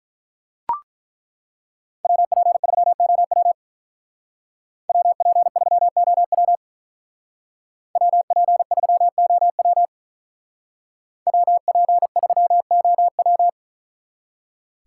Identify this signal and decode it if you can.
Morse code, heard as a keyed tone, about 42 wpm, 700 Hz. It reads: WP3OW WP3OW WP3OW WP3OW
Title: Speed Racing - USA 2x2 Call Signs 35wpm